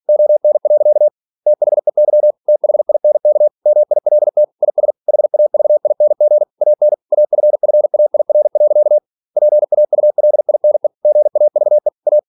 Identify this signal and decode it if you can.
Morse code, heard as a keyed tone, about 46 wpm, 600 Hz. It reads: ON. THEY THINK MILT IS HAVING AN AFFAIR. PAULINE GAVE A
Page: Never 47wpm